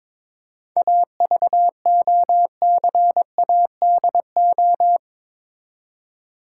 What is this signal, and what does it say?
Morse code, heard as a keyed tone, about 22 wpm, 700 Hz. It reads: AVOCADO
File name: Difficult Words with Hints - 22wpm